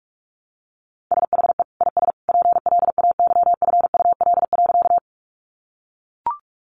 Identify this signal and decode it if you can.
Morse code, heard as a keyed tone, about 44 wpm, 700 Hz. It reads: SHE IS PLAYFUL.
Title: Easy Phrases 45wpm